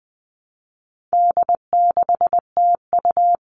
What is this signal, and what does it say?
Morse code, heard as a keyed tone, about 20 wpm, 700 Hz. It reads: D6TU
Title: All Cut Numbers - 4-Digit 20wpm